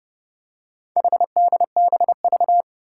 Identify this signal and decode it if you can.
Morse code, heard as a keyed tone, about 30 wpm, 690 Hz. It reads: HDBV
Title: Four Challenging Letter-Number Combinations 30wpm